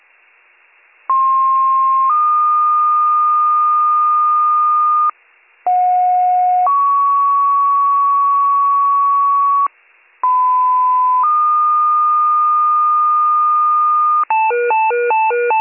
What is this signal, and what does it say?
Two-tone page, heard as a one-second tone, then a three-second tone, about 1060 then 1220 Hz, about 730 then 1120 Hz, about 1010 then 1220 Hz. It is heard from 1.1 to 14.2 s.